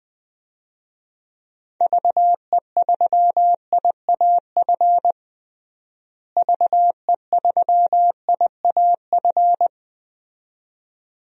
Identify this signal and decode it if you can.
Morse code, heard as a keyed tone, about 21 wpm, 700 Hz. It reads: VE3IAF VE3IAF
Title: DXCC-100 Call Signs - 6 Characters - Repeated 2x - 20wpm